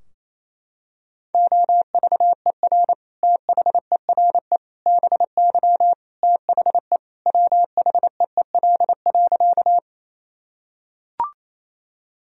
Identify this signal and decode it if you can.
Morse code, heard as a keyed tone, about 28 wpm, 700 Hz. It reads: OVER THERE BY THE WHEEL.
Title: Sentences from Top 1000 Words - Review Part 3 - 28wpm